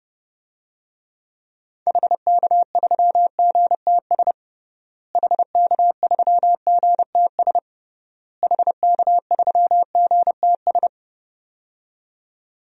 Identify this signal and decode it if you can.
Morse code, heard as a keyed tone, about 32 wpm, 700 Hz. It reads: HK3GTS HK3GTS HK3GTS
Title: DXCC-100 Call Signs - 6 Characters - Repeated 3x - 30wpm